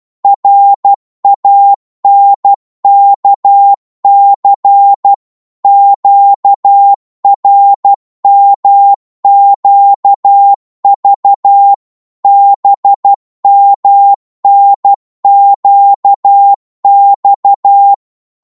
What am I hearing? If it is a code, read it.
Morse code, heard as a keyed tone, about 12 wpm, 800 Hz. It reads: RANKC QRMQV BMNQX
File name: Morse / ltr-12wpm